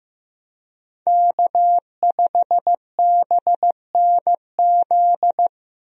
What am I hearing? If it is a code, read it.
Morse code, heard as a keyed tone, about 15 wpm, 700 Hz. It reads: K5BNZ
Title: USA Call Signs - 5 Characters - Repeated 2x - 15wpm